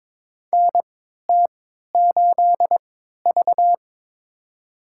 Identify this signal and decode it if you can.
Morse code, heard as a keyed tone, about 22 wpm, spaced as for 12 wpm, 700 Hz. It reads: NT8V
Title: USA Call Signs - 3x Character Spacing - 22wpm